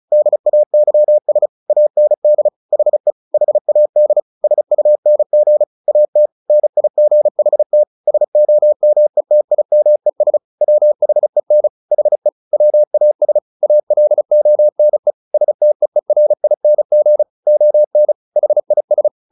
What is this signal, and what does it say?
Morse code, heard as a keyed tone, about 34 wpm, 600 Hz. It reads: DAYS AND HE HAD SUNG AT NIGHT SOMETIMES WHEN HE WAS ALONE STEERING ON HIS
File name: the old man and the sea / the old man and the sea 1 35wpm